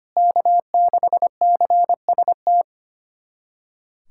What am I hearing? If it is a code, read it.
Morse code, heard as a keyed tone, about 25 wpm, 700 Hz. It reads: K6CST
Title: USA Callsign DE Callsign - 5 Characters - 25wpm